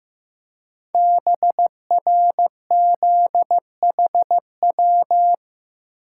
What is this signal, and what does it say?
Morse code, heard as a keyed tone, about 15 wpm, 700 Hz. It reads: BRZHW
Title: Code Groups - 5 Letters - 15wpm